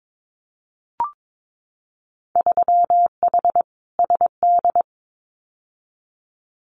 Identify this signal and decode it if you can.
Morse code, heard as a keyed tone, about 22 wpm, 700 Hz. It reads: 3H SD
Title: Winter Field Day Exchanges Encore 22wpm